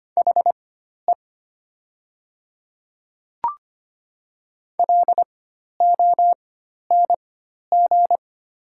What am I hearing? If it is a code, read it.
Morse code, heard as a keyed tone, about 25 wpm, spaced as for 12 wpm, 700 Hz. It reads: HE LONG
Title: 4x Character Spacing - Top 300 Words - 25wpm